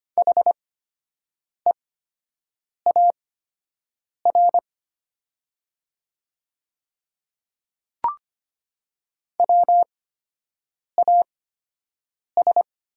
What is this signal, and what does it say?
Morse code, heard as a keyed tone, about 25 wpm, spaced as for 7 wpm, 700 Hz. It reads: HEAR WAS
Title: 8x Character Spacing - Top 300 Words - 25wpm